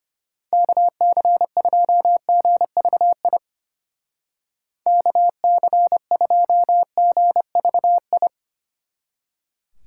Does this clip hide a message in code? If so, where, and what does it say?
Yes: Morse code from 0.5 to 8.3 s, heard as a keyed tone, about 32 wpm, 700 Hz. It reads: KC2GVI KC2GVI